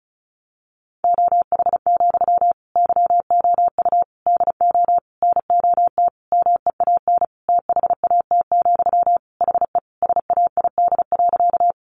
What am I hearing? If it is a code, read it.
Morse code, heard as a keyed tone, about 34 wpm, 700 Hz. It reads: OH, YOU DO NOT MEAN THAT, HE SAID.
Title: Sentences from Top 600 Words 35wpm